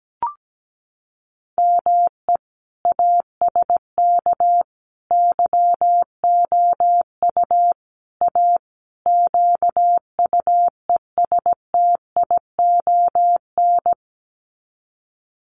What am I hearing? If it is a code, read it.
Morse code, heard as a keyed tone, about 17 wpm, 700 Hz. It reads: ME ASK YOU A QUESTION
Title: Sets of 5 Words Encore2 17wpm